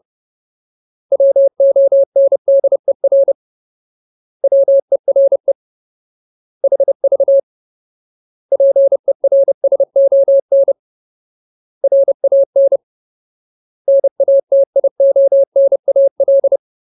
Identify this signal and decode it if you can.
Morse code, heard as a keyed tone, about 30 wpm, 550 Hz. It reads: WONDER WERE HV PERSON RAN NATIONAL
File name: Ultimate Word List / 30WPM Ultimate